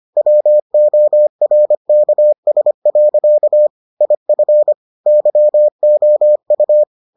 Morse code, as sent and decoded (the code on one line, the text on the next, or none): .-- --- .-. -.- ... .-.-.- / .. ..-. / -.-- --- ..-
WORKS. IF YOU